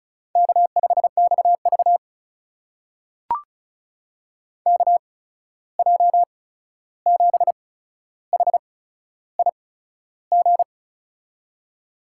Morse code, heard as a keyed tone, about 36 wpm, spaced as for 13 wpm, 700 Hz. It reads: K5XV KJ7HIG
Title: USA Call Signs - 8x Character Spacing - 35wpm